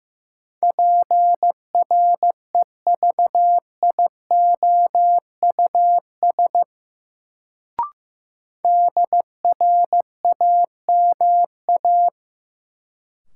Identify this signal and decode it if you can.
Morse code, heard as a keyed tone, about 15 wpm, 700 Hz. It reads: PREVIOUS DRAMA
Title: Sentences from Top 1500 Words - New Word Drill Alt - 15wpm